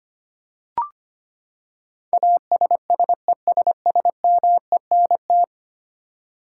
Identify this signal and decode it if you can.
Morse code, heard as a keyed tone, about 25 wpm, 700 Hz. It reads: ASSESSMENT
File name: Ten Letter Words - 25wpm